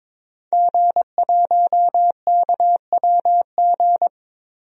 Morse code, heard as a keyed tone, about 21 wpm, 700 Hz. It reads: G1KWG
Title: DXCC-100 Call Signs - 5 Characters - Repeated 2x - 22wpm